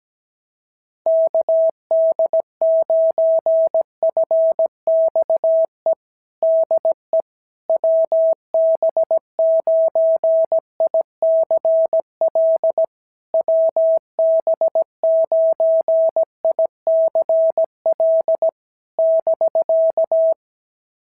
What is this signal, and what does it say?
Morse code, heard as a keyed tone, about 16 wpm, 650 Hz. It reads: KD9FXE DE WB9ICL WB9ICL <BK>
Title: SOTA QSO Practice 17wpm